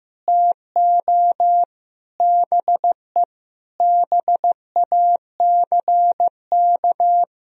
Morse code, heard as a keyed tone, about 15 wpm, 700 Hz. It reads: TO BE BACK
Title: Sentences from Top 1000 Words - Review Part 4 - 15wpm